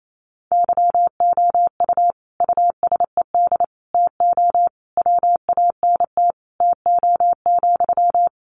Morse code, heard as a keyed tone, about 28 wpm, 700 Hz. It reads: YOU USED TO WANT TO,
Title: Sentences from Top 600 Words 28wpm